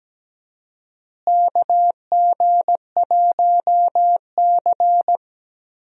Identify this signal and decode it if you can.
Morse code, heard as a keyed tone, about 17 wpm, 700 Hz. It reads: KG1C
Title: USA Call Signs - 4 Characters - Repeated 3x - 17wpm